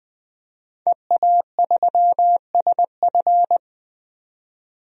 Morse code, heard as a keyed tone, about 20 wpm, 700 Hz. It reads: EA3SF
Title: DXCC-100 Call Signs - 5 Characters - 20wpm